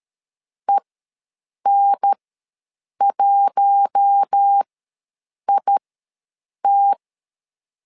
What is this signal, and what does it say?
Morse code, heard as a keyed tone, about 13 wpm, spaced as for 7 wpm, 800 Hz. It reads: EN1IT
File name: LEZIONI / 4 Q1 ws 12wpm 2129